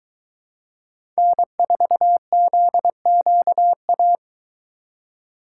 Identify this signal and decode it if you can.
Morse code, heard as a keyed tone, about 23 wpm, 700 Hz. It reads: N4ZQA